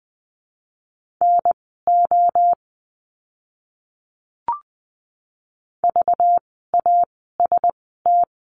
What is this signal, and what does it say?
Morse code, heard as a keyed tone, about 20 wpm, spaced as for 15 wpm, 700 Hz. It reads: NO VAST
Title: 192 Lesson 31f - Letter v - Words 20wpm